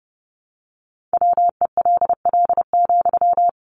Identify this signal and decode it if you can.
Morse code, heard as a keyed tone, about 30 wpm, 700 Hz. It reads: WELL,
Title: Sentences from Top 500 Words 30wpm